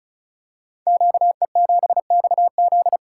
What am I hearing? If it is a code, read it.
Morse code, heard as a keyed tone, about 36 wpm, 700 Hz. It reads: QE7XZ
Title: Code Groups - 5 Letter:Numbers - 35wpm